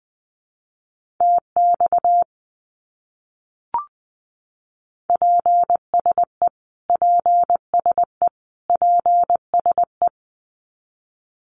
Morse code, heard as a keyed tone, about 20 wpm, 700 Hz. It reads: TX PSE PSE PSE
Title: Abbreviations Repeated 3x - 20wpm